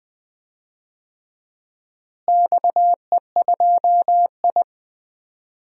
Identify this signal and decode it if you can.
Morse code, heard as a keyed tone, about 20 wpm, 700 Hz. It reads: XE2I
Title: DXCC-100 Call Signs - 4 Characters - Repeated 3x - 20wpm